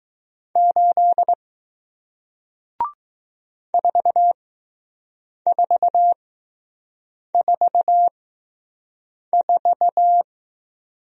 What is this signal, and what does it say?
Morse code, heard as a keyed tone, about 20 wpm, spaced as for 6 wpm, 700 Hz. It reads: 8 4444